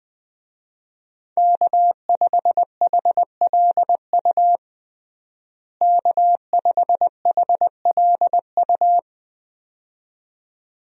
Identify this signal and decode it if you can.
Morse code, heard as a keyed tone, about 21 wpm, 700 Hz. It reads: K5HLU K5HLU
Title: USA Call Signs - 5 Characters - Repeated 2x - 20wpm